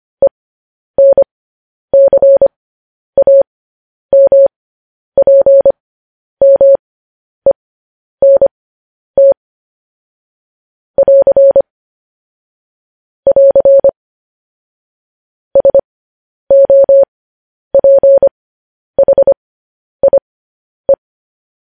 Morse code, heard as a keyed tone, about 25 wpm, spaced as for 10 wpm, 550 Hz. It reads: ENCAMPMENT <AR> <AR> SOPHIE
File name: podcast / News headlines-1x-25x10